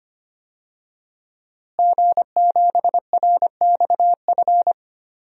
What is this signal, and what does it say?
Morse code, heard as a keyed tone, about 25 wpm, 700 Hz. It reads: G7RXF
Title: DXCC-100 Call Signs - 5 Characters - Repeated 2x - 25wpm